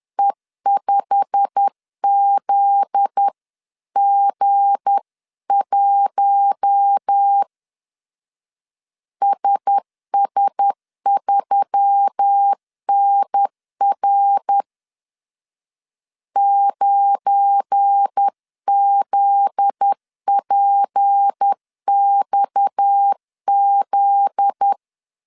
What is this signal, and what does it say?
Morse code, heard as a keyed tone, about 10 wpm, 800 Hz. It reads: E5ZG1 SS3NR 9ZPXZ